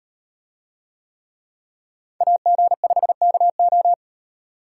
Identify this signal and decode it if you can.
Morse code, heard as a keyed tone, about 36 wpm, 700 Hz. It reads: AG5KO